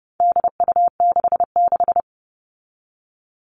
Morse code, heard as a keyed tone, about 30 wpm, 700 Hz. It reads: DU66
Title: All Cut Numbers - 4-Digit 30wpm